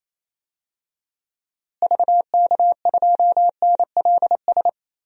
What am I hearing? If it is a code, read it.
Morse code, heard as a keyed tone, about 28 wpm, 700 Hz. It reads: VK2NLS
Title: DXCC-100 Call Signs - 6 Characters - Repeated 3x - 28wpm